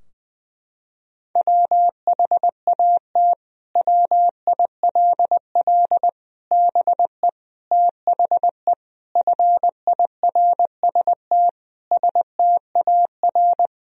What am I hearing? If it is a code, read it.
Morse code, heard as a keyed tone, about 20 wpm, 700 Hz. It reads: WHAT WILL BE THE FIRST STAR
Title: Sentences from Top 1000 Words - Review Part 2 - 20wpm